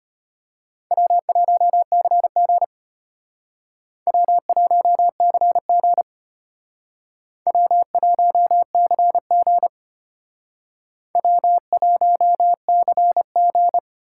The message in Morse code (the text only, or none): W1CG W1CG W1CG W1CG